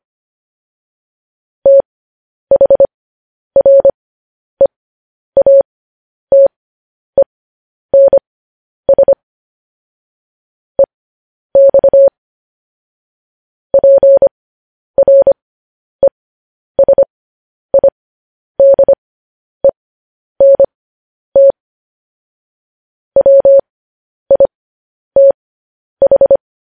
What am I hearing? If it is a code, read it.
Morse code, heard as a keyed tone, about 25 wpm, spaced as for 10 wpm, 550 Hz. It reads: THREATENS EX PRESIDENT WITH